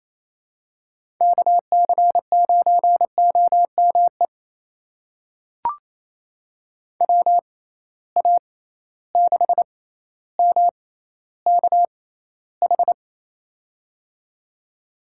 Morse code, heard as a keyed tone, about 28 wpm, spaced as for 14 wpm, 700 Hz. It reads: KC9OME WA6MKH
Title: USA Call Signs - 6 Characters - 6x Character Spacing - 28wpm